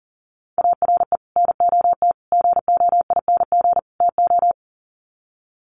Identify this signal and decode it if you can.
Morse code, heard as a keyed tone, about 37 wpm, 700 Hz. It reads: ARE NOT GOING TO
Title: Sets of 4 Words 40wpm